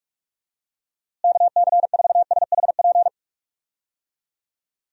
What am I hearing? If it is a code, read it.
Morse code, heard as a keyed tone, about 46 wpm, 700 Hz. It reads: KC4SHP
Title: USA Call Signs - 6 Characters - Repeated 3x - 45wpm